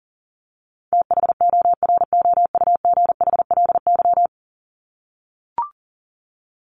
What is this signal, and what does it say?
Morse code, heard as a keyed tone, about 40 wpm, 700 Hz. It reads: THOROUGHLY
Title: LY Suffix - Top 100 Words with Base Words 40wpm